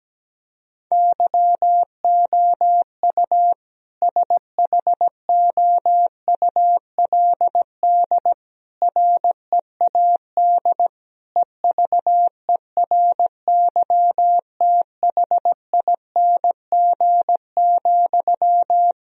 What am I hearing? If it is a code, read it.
Morse code, heard as a keyed tone, about 17 wpm, 700 Hz. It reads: YOU SHOULD READ EVERYTHING,
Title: Sentences from Top 1000 Words - Review Part 4 - 17wpm